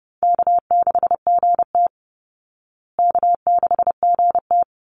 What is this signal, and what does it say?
Morse code, heard as a keyed tone, about 30 wpm, 700 Hz. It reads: K6GT K6GT
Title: Call Signs Repeated 3x - 30wpm